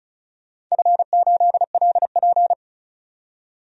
Morse code, heard as a keyed tone, about 36 wpm, 700 Hz. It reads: F8LP
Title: DXCC-100 Call Signs - 4 Characters - 35wpm